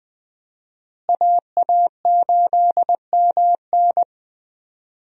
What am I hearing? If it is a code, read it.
Morse code, heard as a keyed tone, about 21 wpm, 700 Hz. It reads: AA8MN
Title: USA Call Signs - 5 Characters - Repeated 3x - 20wpm